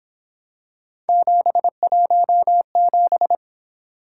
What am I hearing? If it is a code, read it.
Morse code, heard as a keyed tone, about 26 wpm, 700 Hz. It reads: 717
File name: Speed-Racing - Three Letter-Number Combinations - 22wpm